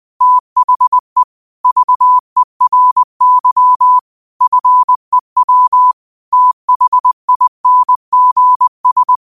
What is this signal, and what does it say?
Morse code, heard as a keyed tone, about 20 wpm, 1000 Hz. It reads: THE VERY FEW THINGS